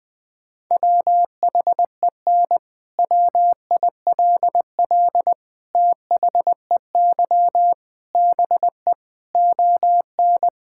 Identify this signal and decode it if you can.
Morse code, heard as a keyed tone, about 20 wpm, 700 Hz. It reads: WHEN WILL THEY BE ON
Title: Sentences from Top 1000 Words - Review Part 6 - 20wpm